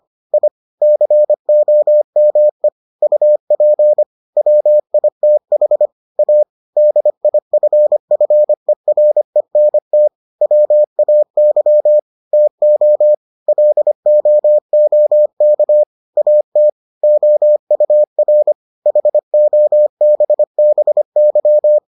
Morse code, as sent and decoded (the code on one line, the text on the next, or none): .. / -.-. --- -- . / ..- .--. / .-- .. - .... / .- / -.. .. ..-. ..-. .-. -. - / .-- .- -.-- / - --- / .-.. --- --- -.- / .- - / --- ..- .-. / .... --- -... -... -.--
I COME UP WITH A DIFFERENT WAY TO LOOK AT OUR HOBBY